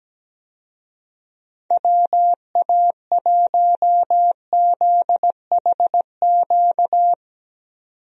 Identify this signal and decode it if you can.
Morse code, heard as a keyed tone, about 17 wpm, 700 Hz. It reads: WA1ZHQ
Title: USA Call Signs - 6 Characters - 17wpm